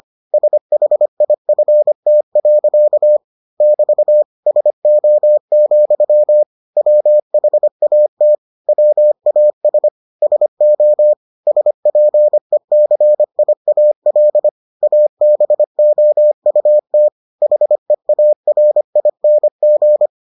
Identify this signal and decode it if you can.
Morse code, heard as a keyed tone, about 25 wpm, 600 Hz. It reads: SHIFT. <BT> SO, WHAT WAS SO SPECIAL ABOUT HEARING